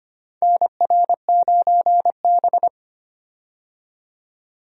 Morse code, heard as a keyed tone, about 25 wpm, 700 Hz. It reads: NR9B
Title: USA Call Signs - 4 Characters - 25wpm